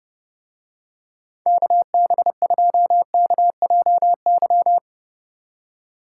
Morse code, heard as a keyed tone, about 30 wpm, 700 Hz. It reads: KB2KJY